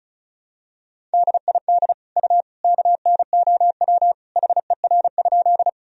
Morse code, heard as a keyed tone, about 36 wpm, 700 Hz. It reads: DID U KNOW HER?
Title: Sentences from Top 100 Words and Abbreviations 35wpm